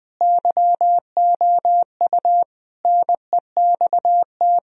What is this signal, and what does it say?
Morse code, heard as a keyed tone, about 21 wpm, 700 Hz. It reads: YOU NEXT
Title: Sentences from Top 1000 Words - Review Part 1 - 20wpm